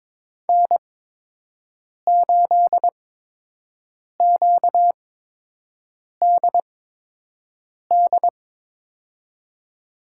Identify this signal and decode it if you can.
Morse code, heard as a keyed tone, about 22 wpm, spaced as for 6 wpm, 700 Hz. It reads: N8QDD
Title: USA Call Signs - 5 Characters - 8x Character Spacing - 22wpm